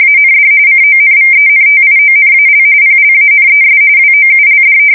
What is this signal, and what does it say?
RTTY, about 45.45 baud. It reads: AND CONVENTIONAL PLANES WOULD
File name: fsk170 / battleshipnj